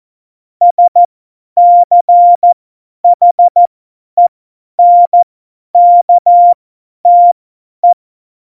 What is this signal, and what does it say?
Morse code, heard as a keyed tone, about 14 wpm, spaced as for 10 wpm, 700 Hz. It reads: SCHENKTE
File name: Morsekurs / Witze / Witz 18. Februar 2020, 50 F70